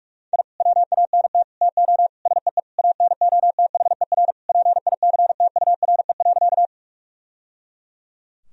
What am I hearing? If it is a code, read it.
Morse code, heard as a keyed tone, about 46 wpm, 700 Hz. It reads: I WANT TO SEE ANOTHER PICTURE.
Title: Sentences from Top 1000 Words - Review Part 1 - 45wpm